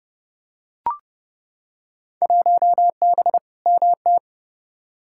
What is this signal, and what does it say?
Morse code, heard as a keyed tone, about 30 wpm, 700 Hz. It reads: 1B MT
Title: Field Day Exchanges - 30wpm